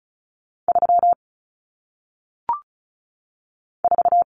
Morse code, heard as a keyed tone, about 36 wpm, 700 Hz. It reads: 3 4
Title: Numbers - 1-Digit 35wpm